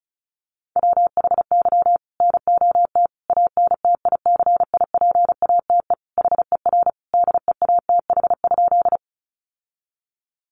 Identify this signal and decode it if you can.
Morse code, heard as a keyed tone, about 34 wpm, 700 Hz. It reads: WHY NOT ANTICIPATE HER DEATH?
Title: Sentences from Top 1500 Words 35wpm